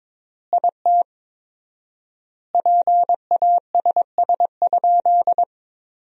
Morse code, heard as a keyed tone, about 22 wpm, 700 Hz. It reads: IT PASS?
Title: 4x Word-Spacing - Sentences from Top 400 Words - 22wpm